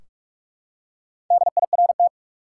Morse code, heard as a keyed tone, about 46 wpm, 700 Hz. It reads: DIRT
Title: Four Letter Words - 45wpm